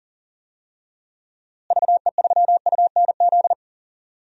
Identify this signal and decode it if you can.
Morse code, heard as a keyed tone, about 40 wpm, 700 Hz. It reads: VE3UNZ
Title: DXCC-100 Call Signs - 6 Characters - Repeated 2x - 40wpm